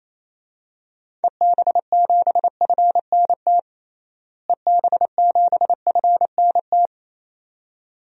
Morse code, heard as a keyed tone, about 27 wpm, 700 Hz. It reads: EB7FNT EB7FNT
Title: DXCC-100 Call Signs - 6 Characters - Repeated 2x - 28wpm